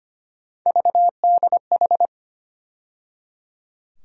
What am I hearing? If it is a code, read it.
Morse code, heard as a keyed tone, about 25 wpm, 700 Hz. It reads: VDH
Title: USA Call Sign Postfixes - 3 Letters 25wpm